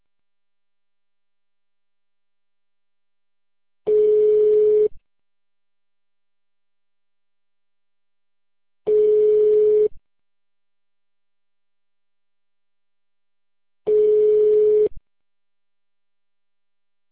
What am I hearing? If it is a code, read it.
Ringback tone (Europe, 425 Hz).